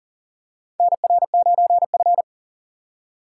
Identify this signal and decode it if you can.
Morse code, heard as a keyed tone, about 40 wpm, 700 Hz. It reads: NR9F
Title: USA Call Signs - 4 Characters - 2x Character Spacing - 40wpm